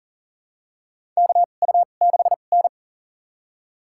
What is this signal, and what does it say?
Morse code, heard as a keyed tone, about 40 wpm, spaced as for 29 wpm, 690 Hz. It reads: KU6N
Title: USA Call Signs - 4 Characters - 2x Character Spacing - 40wpm